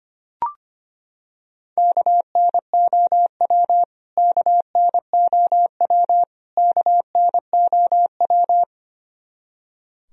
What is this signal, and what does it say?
Morse code, heard as a keyed tone, about 25 wpm, 700 Hz. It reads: KNOW KNOW KNOW
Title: Top 100 Words Repeated 25wpm